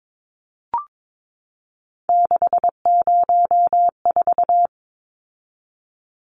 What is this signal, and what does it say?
Morse code, heard as a keyed tone, about 22 wpm, 700 Hz. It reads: 604
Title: Numbers - 3-Digit 22wpm